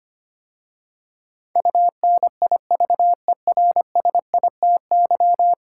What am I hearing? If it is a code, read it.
Morse code, heard as a keyed tone, about 25 wpm, 700 Hz. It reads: UNIVERSITY
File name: 10 Letter Words - First 3 Letters Given - 25wpm